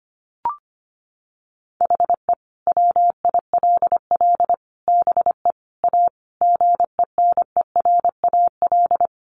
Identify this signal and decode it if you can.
Morse code, heard as a keyed tone, about 25 wpm, 700 Hz. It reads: HE WILL BE A GENERAL
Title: Sentences from Top 1200 Words 25wpm